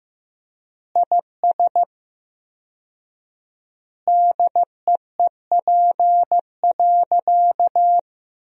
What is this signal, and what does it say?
Morse code, heard as a keyed tone, about 15 wpm, 700 Hz. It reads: IS DEEP.
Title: 4x Word-Spacing - Sentences from Top 500 Words - 15wpm